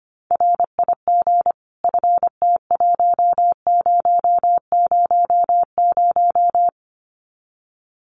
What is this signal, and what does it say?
Morse code, heard as a keyed tone, about 25 wpm, 700 Hz. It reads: RIG FT1000
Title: Common QSO Phrases 25wpm